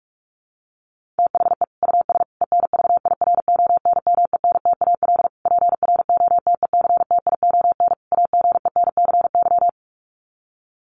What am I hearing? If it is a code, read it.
Morse code, heard as a keyed tone, about 44 wpm, 700 Hz. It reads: THE US ENVIRONMENTAL PROTECTION AGENCY
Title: Sets of 5 Words Encore2 45wpm